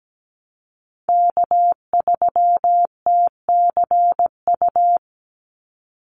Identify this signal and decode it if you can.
Morse code, heard as a keyed tone, about 17 wpm, 700 Hz. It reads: K3TCU
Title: NA SOTA Chaser Callsigns 17wpm